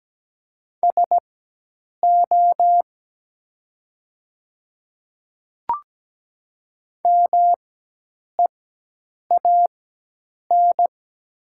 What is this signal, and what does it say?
Morse code, heard as a keyed tone, about 17 wpm, spaced as for 8 wpm, 700 Hz. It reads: SO MEAN
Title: 4x Character Spacing - Top 200 Words - 17wpm